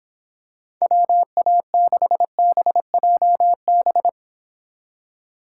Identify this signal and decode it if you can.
Morse code, heard as a keyed tone, about 26 wpm, 700 Hz. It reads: WA6BJB